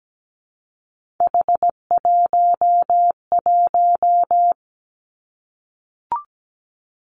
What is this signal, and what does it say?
Morse code, heard as a keyed tone, about 17 wpm, 700 Hz. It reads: H11
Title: Three Challenging Letter-Number Combinations 17wpm